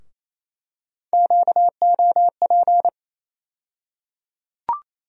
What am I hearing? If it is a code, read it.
Morse code, heard as a keyed tone, about 28 wpm, 700 Hz. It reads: QOP
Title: USA Call Sign Postfixes - 3 Letters 28wpm